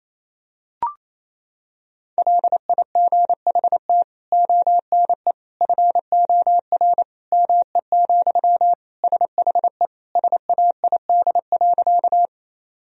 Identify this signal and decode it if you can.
Morse code, heard as a keyed tone, about 27 wpm, 700 Hz. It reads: LIGHT ONE FOR ME, SHE SAID.
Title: Sentences from Top 1000 Words - Review Part 1 - 28wpm